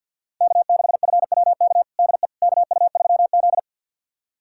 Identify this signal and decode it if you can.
Morse code, heard as a keyed tone, about 45 wpm, 700 Hz. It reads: K6FWK DE KU3Z
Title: USA Callsign DE Callsign 50wpm